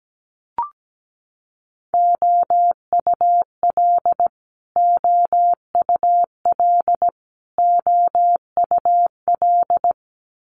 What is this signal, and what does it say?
Morse code, heard as a keyed tone, about 17 wpm, 700 Hz. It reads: OUL OUL OUL
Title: Multi-letter Phonemes Repeated 3x 17wpm